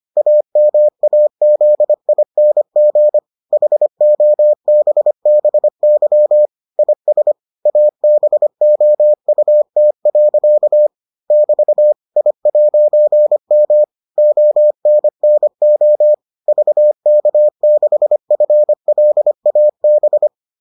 Morse code, heard as a keyed tone, about 25 wpm, 600 Hz. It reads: AMAZING HOBBY IS ABOUT. <BT> I'M ONNO VK6FLAB